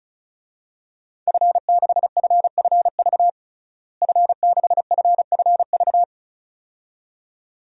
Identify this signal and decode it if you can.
Morse code, heard as a keyed tone, about 36 wpm, 700 Hz. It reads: F6FFV F6FFV
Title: DXCC-100 Call Signs - 5 Characters - Repeated 2x - 35wpm